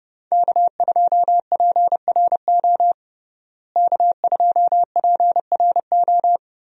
Morse code, heard as a keyed tone, about 32 wpm, 700 Hz. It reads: K2PRO K2PRO
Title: USA Call Signs - 5 Characters - Repeated 3x - 30wpm